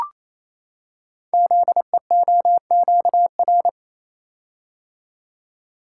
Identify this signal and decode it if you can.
Morse code, heard as a keyed tone, about 28 wpm, 700 Hz. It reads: ZEOQR
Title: Code Groups - 5 Letters - 28wpm